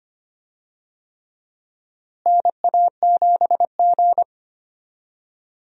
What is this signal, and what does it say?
Morse code, heard as a keyed tone, about 25 wpm, 700 Hz. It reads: NA7G
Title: USA Call Signs - 4 Characters - Repeated 3x - 25wpm